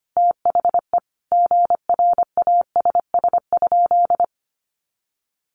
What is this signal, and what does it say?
Morse code, heard as a keyed tone, about 25 wpm, 700 Hz. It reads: THE GRASS?
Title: A Sound Of Thunder 25wpm